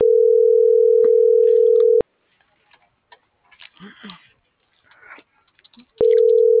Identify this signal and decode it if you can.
Ringback tone (North America), 440 + 480 Hz.